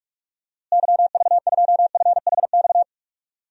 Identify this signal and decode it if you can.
Morse code, heard as a keyed tone, about 46 wpm, 700 Hz. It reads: YV2FHX
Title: DXCC-100 Call Signs - 6 Characters - Repeated 2x - 45wpm